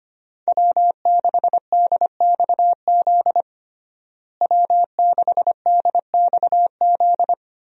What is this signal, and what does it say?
Morse code, heard as a keyed tone, about 25 wpm, 700 Hz. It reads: W6DXZ W6DXZ
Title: USA Call Signs - 5 Characters - Repeated 2x - 25wpm